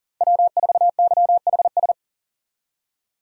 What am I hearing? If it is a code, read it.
Morse code, heard as a keyed tone, about 40 wpm, 700 Hz. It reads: W4YHS
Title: USA Call Signs - 5 Characters - 3x Character Spacing - 40wpm